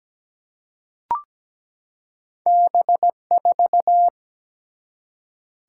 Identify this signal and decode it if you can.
Morse code, heard as a keyed tone, about 17 wpm, 700 Hz. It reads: B4
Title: Single-Letter Single-Number 17wpm